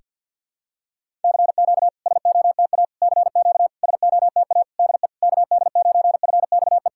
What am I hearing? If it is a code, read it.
Morse code, heard as a keyed tone, about 53 wpm, 690 Hz. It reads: CQ SOTA CQ SOTA DE KD9FXE
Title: SOTA QSO Practice 50wpm